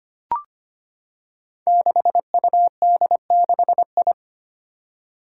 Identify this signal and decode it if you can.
Morse code, heard as a keyed tone, about 25 wpm, 700 Hz. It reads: 6UD6I
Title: Five Challenging Letter-Number Combinations 25wpm